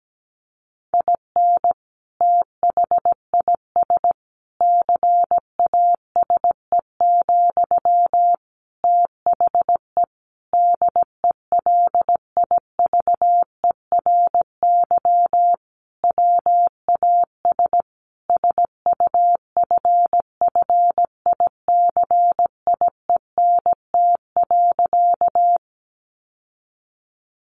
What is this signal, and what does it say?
Morse code, heard as a keyed tone, about 17 wpm, 700 Hz. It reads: IN THIS CASE, THE DELIVERY WAS SUFFICIENT.
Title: Sentences from Top 1500 Words 17wpm